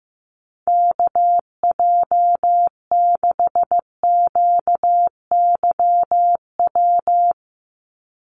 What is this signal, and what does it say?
Morse code, heard as a keyed tone, about 15 wpm, 700 Hz. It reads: KJ6QYW
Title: USA Call Signs 15wpm